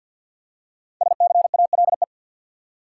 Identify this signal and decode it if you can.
Morse code, heard as a keyed tone, about 52 wpm, 700 Hz. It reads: SCALE
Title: Five Letter Words - 50wpm